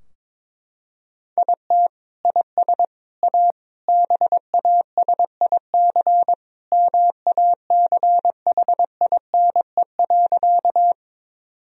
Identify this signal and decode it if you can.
Morse code, heard as a keyed tone, about 22 wpm, 700 Hz. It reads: IT IS A BASIC MACHINE.
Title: Sentences from Top 1000 Words - Review Part 3 - 22wpm